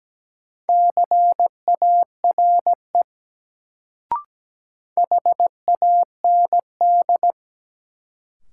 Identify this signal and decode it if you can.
Morse code, heard as a keyed tone, about 17 wpm, 700 Hz. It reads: CARE HAND
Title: Sets of 3 Words - New Word Drill - 17wpm